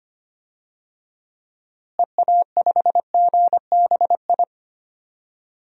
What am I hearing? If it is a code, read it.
Morse code, heard as a keyed tone, about 25 wpm, 700 Hz. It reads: EA5GBI
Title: DXCC-100 Call Signs - 6 Characters - Repeated 3x - 25wpm